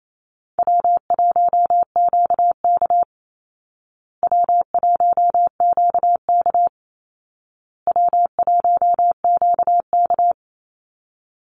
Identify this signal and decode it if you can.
Morse code, heard as a keyed tone, about 28 wpm, 700 Hz. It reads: W1QK W1QK W1QK
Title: Easy Call Signs Repeated 3x - 28wpm